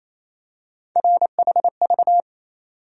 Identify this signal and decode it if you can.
Morse code, heard as a keyed tone, about 28 wpm, 700 Hz. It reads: RHV
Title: USA Call Sign Postfixes - 3 Letters 28wpm